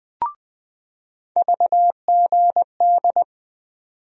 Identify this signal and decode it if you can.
Morse code, heard as a keyed tone, about 20 wpm, 700 Hz. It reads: VGD
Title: USA Call Sign Postfixes - 3 Letters 20wpm